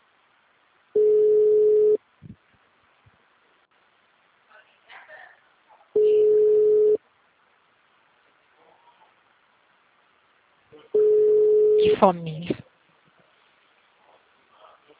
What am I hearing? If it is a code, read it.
Ringback tone (Europe, 425 Hz), with speech.
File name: out- -2022- - - .32571